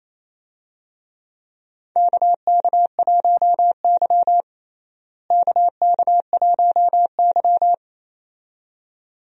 Morse code, heard as a keyed tone, about 28 wpm, 700 Hz. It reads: KK1Y KK1Y
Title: USA Call Signs - 4 Characters - Repeated 2x - 28wpm